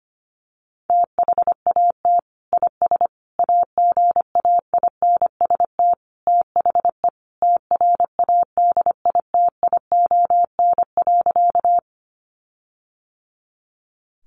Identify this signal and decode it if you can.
Morse code, heard as a keyed tone, about 25 wpm, 700 Hz. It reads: THAT IS AGAINST THE TRADITION.
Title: Sentences from Top 1500 Words 25wpm